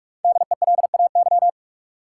Morse code, heard as a keyed tone, about 46 wpm, 700 Hz. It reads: DELAY
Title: Five Letter Words - 45wpm